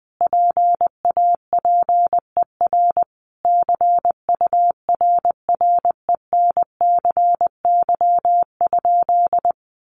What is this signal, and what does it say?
Morse code, heard as a keyed tone, about 20 wpm, 700 Hz. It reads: PAPER CURRENCY?
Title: Sentences from Top 1500 Words 20wpm